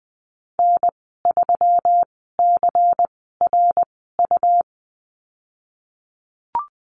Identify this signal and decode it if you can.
Morse code, heard as a keyed tone, about 20 wpm, spaced as for 15 wpm, 700 Hz. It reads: N3CRU
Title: 147 Lesson 24g - Letter y - Call Signs 20wpm